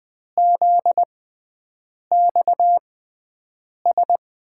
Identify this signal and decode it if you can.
Morse code, heard as a keyed tone, about 20 wpm, spaced as for 7 wpm, 700 Hz. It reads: ZXS
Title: USA Call Signs - 6 Characters - 6x Character Spacing - 20wpm